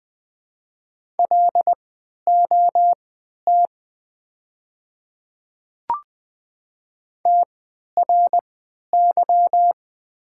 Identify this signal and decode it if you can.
Morse code, heard as a keyed tone, about 20 wpm, spaced as for 11 wpm, 700 Hz. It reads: LOT TRY
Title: 3x Character Spacing - Top 500 Words - 20wpm